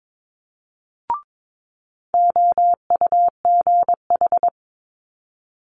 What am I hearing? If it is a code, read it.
Morse code, heard as a keyed tone, about 22 wpm, 700 Hz. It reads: OUGH
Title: Multi-letter Phonemes Quick Fire 22wpm